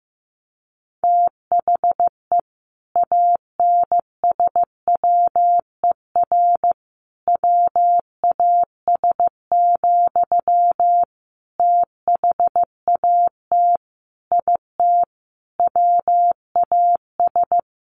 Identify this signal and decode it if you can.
Morse code, heard as a keyed tone, about 15 wpm, 700 Hz. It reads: THE ANSWER WAS, THAT IT WAS
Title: Sentences from Top 1100 Words 15wpm